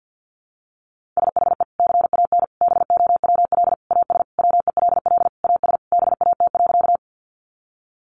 Morse code, heard as a keyed tone, about 50 wpm, 700 Hz. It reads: SHE CAN BOWL AS WELL AS BAT.
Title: Sentences from Top 1500 Words 50wpm